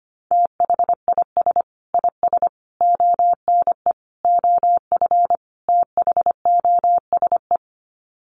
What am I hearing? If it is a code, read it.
Morse code, heard as a keyed tone, about 25 wpm, 700 Hz. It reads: THIS IS ONE OF THOSE
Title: Sets of 5 Words Encore2 25wpm